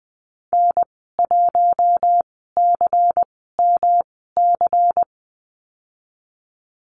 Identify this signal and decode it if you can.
Morse code, heard as a keyed tone, about 20 wpm, spaced as for 15 wpm, 700 Hz. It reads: N1CMC
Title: 140 Lesson 23g - Letter f - Call Signs 20wpm